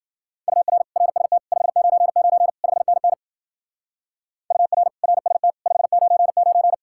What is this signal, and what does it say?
Morse code, heard as a keyed tone, about 49 wpm, 700 Hz. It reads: UR RST 599 5NN UR RST 599
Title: Speed-Racing Common QSO Phrases - 40wpm